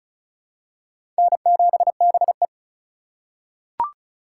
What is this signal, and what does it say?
Morse code, heard as a keyed tone, about 36 wpm, 700 Hz. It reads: N7BE
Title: USA Call Signs - 4 Characters - 3x Character Spacing - 35wpm